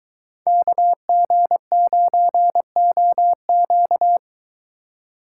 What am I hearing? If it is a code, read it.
Morse code, heard as a keyed tone, about 22 wpm, 700 Hz. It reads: KG9OQ